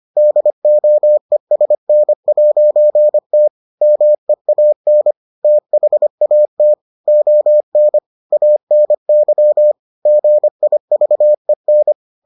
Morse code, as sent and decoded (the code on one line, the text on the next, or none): -.. --- . ... -. .----. - / -- . .- -. / - .... .- - / --- -. / .- -. -.-- / --. .. ...- . -.
DOESN'T MEAN THAT ON ANY GIVEN